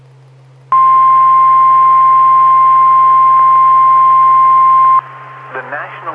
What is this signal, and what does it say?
Emergency Alert System (SAME) tones (the 1050 Hz attention tone), with speech. It is heard from 0.7 to 5.0 s.